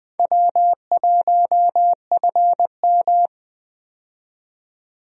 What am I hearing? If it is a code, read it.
Morse code, heard as a keyed tone, about 20 wpm, 700 Hz. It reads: W1FM
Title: USA Call Signs - 4 Characters - 20wpm